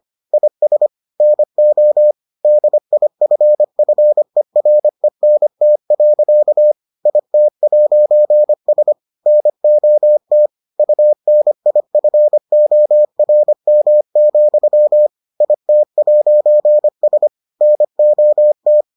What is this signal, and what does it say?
Morse code, heard as a keyed tone, about 25 wpm, 600 Hz. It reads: IS NO DIFFERENT. IT'S NOT UNIFORM, IT'S NOT